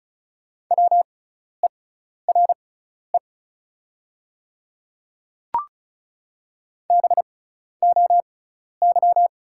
Morse code, heard as a keyed tone, about 34 wpm, spaced as for 12 wpm, 690 Hz. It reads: WERE BOY
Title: 6x Character Spacing - Top 200 Words - 35wpm